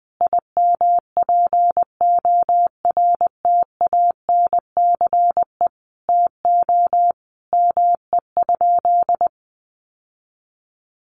Morse code, heard as a keyed tone, about 20 wpm, 700 Hz. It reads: IMPORTANCE TO ME?
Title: Sentences from Top 1200 Words 20wpm